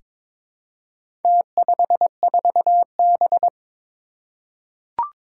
Morse code, heard as a keyed tone, about 22 wpm, 700 Hz. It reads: T54B
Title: Four Challenging Letter-Number Combinations 22wpm